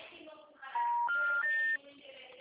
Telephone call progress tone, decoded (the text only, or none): special information tone